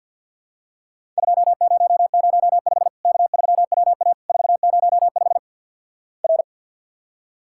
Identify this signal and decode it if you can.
Morse code, heard as a keyed tone, about 52 wpm, 700 Hz. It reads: 2005 K3WA 405 R
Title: Worked All Europe DX Contest - QTC Exchange - 50wpm